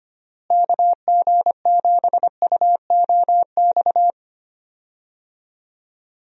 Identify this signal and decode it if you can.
Morse code, heard as a keyed tone, about 25 wpm, 700 Hz. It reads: KG7UOX
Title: USA Call Signs - 6 Characters - Repeated 2x - 25wpm